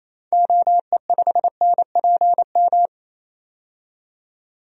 Morse code, heard as a keyed tone, about 28 wpm, 700 Hz. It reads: OE5NPM
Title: DXCC-100 Call Signs - 6 Characters - 28wpm